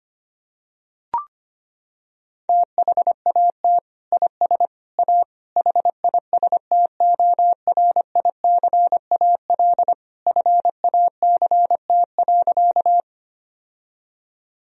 Morse code, heard as a keyed tone, about 25 wpm, 700 Hz. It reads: THAT IS A HISTORICAL FACT.
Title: Sentences from Top 1000 Words - Review Part 4 - 25wpm